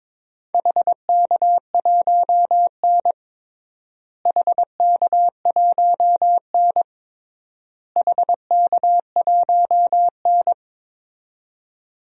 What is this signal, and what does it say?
Morse code, heard as a keyed tone, about 22 wpm, 700 Hz. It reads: HK1N HK1N HK1N
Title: DXCC-100 Call Signs - 4 Characters - Repeated 3x - 22wpm